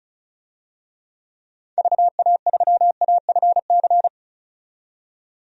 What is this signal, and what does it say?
Morse code, heard as a keyed tone, about 34 wpm, 700 Hz. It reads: VA3AFC
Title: DXCC-100 Call Signs - 6 Characters - 35wpm